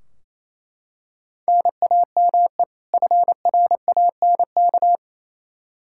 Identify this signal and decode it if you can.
Morse code, heard as a keyed tone, about 27 wpm, 700 Hz. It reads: NAME FRANK
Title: K1USN SST QSO Practice 28wpm